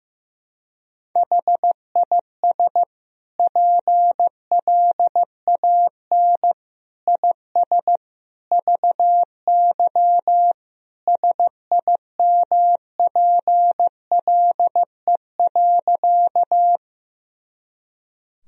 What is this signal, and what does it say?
Morse code, heard as a keyed tone, about 15 wpm, 700 Hz. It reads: HIS PLAN IS VY SIMPLE.
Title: Sentences from Top 500 Words and Abbreviations 15wpm